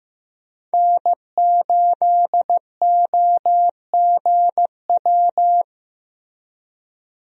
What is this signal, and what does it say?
Morse code, heard as a keyed tone, about 15 wpm, 700 Hz. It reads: N8OGW
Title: USA Call Signs - 5 Characters - Repeated 2x - 15wpm